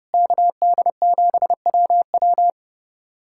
Morse code, heard as a keyed tone, about 30 wpm, 700 Hz. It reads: KD7WW
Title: USA Callsign DE Callsign 30wpm